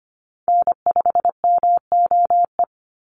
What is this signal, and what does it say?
Morse code, heard as a keyed tone, about 25 wpm, 700 Hz. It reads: N5MOE
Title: USA Call Signs 25wpm